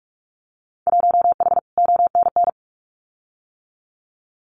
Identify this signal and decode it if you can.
Morse code, heard as a keyed tone, about 44 wpm, 700 Hz. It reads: 1H ONN
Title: Winter Field Day Exchanges 45wpm